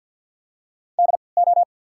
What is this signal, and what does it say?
Morse code, heard as a keyed tone, about 53 wpm, spaced as for 29 wpm, 690 Hz. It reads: DO
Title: 3x Character Spacing - Top 100 Words - 50wpm